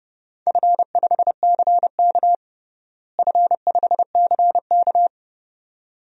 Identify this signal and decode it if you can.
Morse code, heard as a keyed tone, about 30 wpm, 700 Hz. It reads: F5CK F5CK
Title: DXCC-100 Call Signs - 4 Characters - Repeated 2x - 30wpm